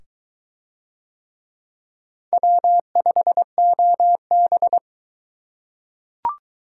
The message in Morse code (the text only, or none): W5OB